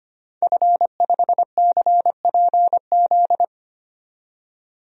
Morse code, heard as a keyed tone, about 25 wpm, 700 Hz. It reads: F5CPZ
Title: DXCC-100 Call Signs - 5 Characters - 25wpm